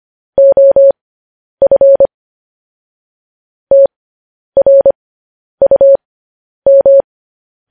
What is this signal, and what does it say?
Morse code, heard as a keyed tone, about 25 wpm, spaced as for 10 wpm, 550 Hz. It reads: OF TRUM